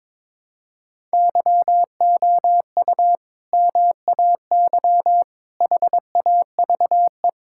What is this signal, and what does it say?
Morse code, heard as a keyed tone, about 22 wpm, 700 Hz. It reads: YOU MAY HAVE
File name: practice / 22wpm / Sentences with 2-Word Phrases - 22wpm